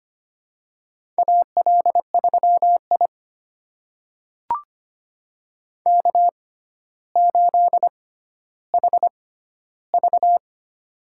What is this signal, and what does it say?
Morse code, heard as a keyed tone, about 25 wpm, spaced as for 13 wpm, 700 Hz. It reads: AL3I K8HV
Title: USA Call Signs - 4 Characters - 6x Character Spacing - 25wpm